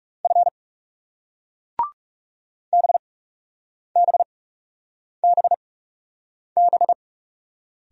Morse code, heard as a keyed tone, about 39 wpm, spaced as for 8 wpm, 700 Hz. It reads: F BBBB